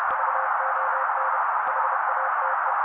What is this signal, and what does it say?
Morse code, heard as a keyed tone, about 30 wpm, 530 Hz. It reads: VON HATE
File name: daily / 29wpm noise